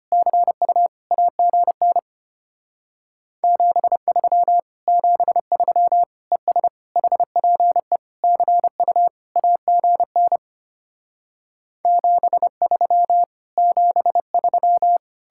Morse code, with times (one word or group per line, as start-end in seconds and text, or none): 0.1-0.9 CU
1.1-2.0 AGN
3.4-4.6 73
4.9-6.0 73
6.3-6.7 ES
7.0-8.0 HPE
8.2-9.1 CU
9.4-10.4 AGN
11.9-13.2 73
13.6-15.0 73